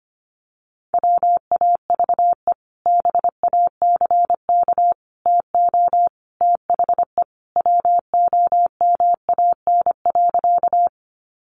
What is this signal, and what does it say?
Morse code, heard as a keyed tone, about 25 wpm, 700 Hz. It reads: WAVE BACK TO THE WOMAN.
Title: Sentences from Top 1400 Words 25wpm